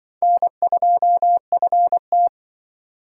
Morse code, heard as a keyed tone, about 23 wpm, 700 Hz. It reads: N2FT
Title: Speed Racing - USA 1x2 Call Signs 20wpm